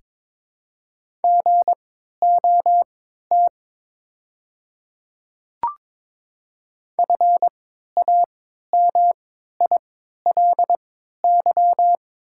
Morse code, heard as a keyed tone, about 21 wpm, spaced as for 12 wpm, 700 Hz. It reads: GOT FAMILY
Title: 3x Character Spacing - Top 500 Words - 22wpm